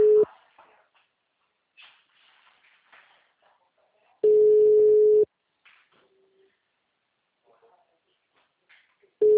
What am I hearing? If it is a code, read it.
Ringback tone (Europe, 425 Hz).
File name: out- -2021- - - .12